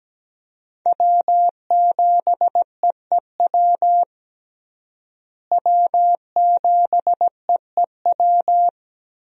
Morse code, heard as a keyed tone, about 17 wpm, 700 Hz. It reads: W7EEW W7EEW
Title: USA Call Signs - 5 Characters - Repeated 2x - 17wpm